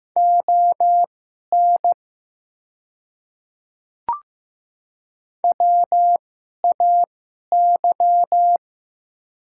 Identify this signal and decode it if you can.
Morse code, heard as a keyed tone, about 15 wpm, spaced as for 11 wpm, 700 Hz. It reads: ON WAY
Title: 2x Character Spacing - Top 100 Words - 15wpm